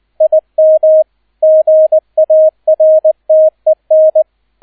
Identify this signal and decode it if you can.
Morse code, heard as a keyed tone, about 20 wpm, 630 Hz. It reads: IM GARTEN
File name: 002 Zweiter Streich 20wpm